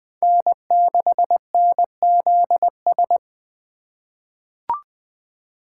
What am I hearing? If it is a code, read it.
Morse code, heard as a keyed tone, about 20 wpm, 700 Hz. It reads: N6NZS
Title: USA Call Signs - 5 Characters - 4x Character Spacing - 20wpm